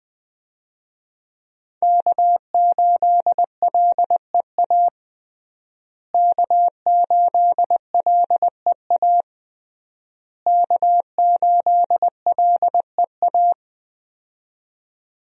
Morse code, heard as a keyed tone, about 21 wpm, 700 Hz. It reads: K8LEA K8LEA K8LEA
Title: USA Call Signs - 5 Characters - Repeated 3x - 20wpm